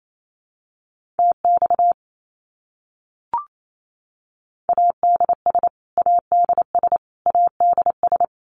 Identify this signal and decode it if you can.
Morse code, heard as a keyed tone, about 28 wpm, 700 Hz. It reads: TX ADS ADS ADS
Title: Abbreviations Repeated 3x - 28wpm